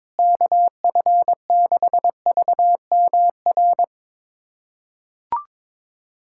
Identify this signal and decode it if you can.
Morse code, heard as a keyed tone, about 22 wpm, 700 Hz. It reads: KF6VMR
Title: USA Callsign DE Callsign - 6 Characters - 22wpm